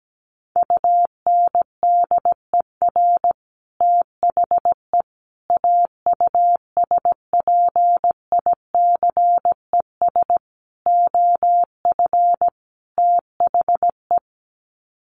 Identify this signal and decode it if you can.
Morse code, heard as a keyed tone, about 17 wpm, 700 Hz. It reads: UNDER THE AUSPICES OF THE
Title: Sets of 5 Words Encore2 17wpm